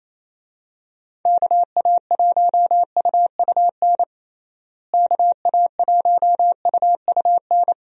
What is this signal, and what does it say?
Morse code, heard as a keyed tone, about 27 wpm, 700 Hz. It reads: KA1UUN KA1UUN
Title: USA Call Signs - 6 Characters - Repeated 2x - 28wpm